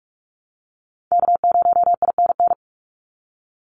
Morse code, heard as a keyed tone, about 44 wpm, 700 Hz. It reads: K0INN
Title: Call Signs Repeated 2x - 45wpm